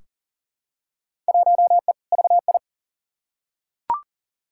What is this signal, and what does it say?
Morse code, heard as a keyed tone, about 40 wpm, 700 Hz. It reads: 1E VI
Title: Field Day Exchanges - 40wpm